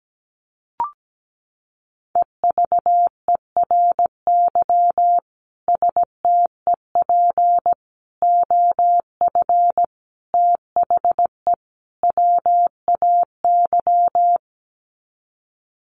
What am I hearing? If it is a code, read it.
Morse code, heard as a keyed tone, about 17 wpm, 700 Hz. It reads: EVERY STEP OF THE WAY
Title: Sets of 5 Words Encore2 17wpm